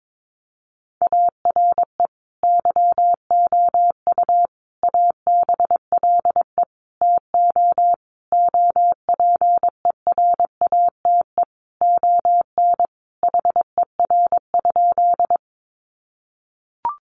Morse code, heard as a keyed tone, about 22 wpm, 700 Hz. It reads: ARE YOU ABLE TO OPERATE ON HER?
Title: Sentences from Top 1500 Words 22wpm